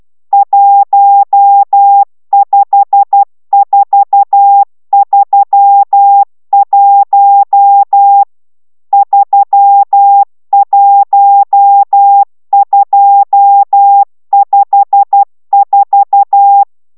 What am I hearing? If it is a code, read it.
Morse code, heard as a keyed tone, about 12 wpm, 800 Hz. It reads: 15431 31254